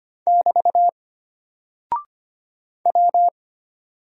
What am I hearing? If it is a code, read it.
Morse code, heard as a keyed tone, about 25 wpm, 700 Hz. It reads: <BT> W
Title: Single Character (Letter:Number:Punctuation:Prosign) Alt 25wpm